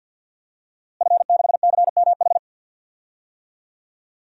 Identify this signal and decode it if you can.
Morse code, heard as a keyed tone, about 52 wpm, 700 Hz. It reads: F6CMH
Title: DXCC-100 Call Signs - 5 Characters - Repeated 3x - 50wpm